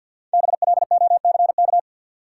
Morse code, heard as a keyed tone, about 51 wpm, 700 Hz. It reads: BLOCK
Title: Five Letter Words - 50wpm